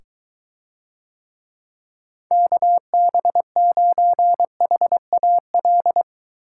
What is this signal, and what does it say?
Morse code, heard as a keyed tone, about 24 wpm, 700 Hz. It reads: KB9HAL